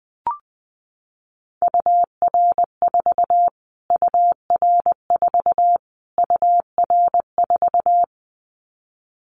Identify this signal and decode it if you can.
Morse code, heard as a keyed tone, about 20 wpm, 700 Hz. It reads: UR4 UR4 UR4
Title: Callsign Prefixes for Top 25 DXCC - Repeated 3x - 20wpm